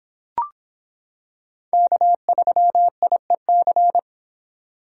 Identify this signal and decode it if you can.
Morse code, heard as a keyed tone, about 26 wpm, 700 Hz. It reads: K3IEC